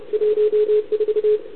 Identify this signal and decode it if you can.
Morse code, heard as a keyed tone, about 33 wpm, 440 Hz. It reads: 14
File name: CQ160 2022 CW / SP